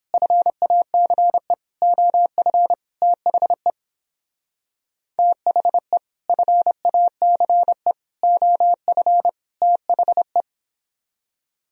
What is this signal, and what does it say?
Morse code, heard as a keyed tone, about 29 wpm, 700 Hz. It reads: FACE OF THE THE FACE OF THE